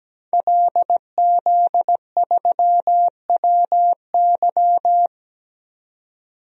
Morse code, heard as a keyed tone, about 17 wpm, 700 Hz. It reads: LZ3WY
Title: DXCC-100 Call Signs - 5 Characters - 17wpm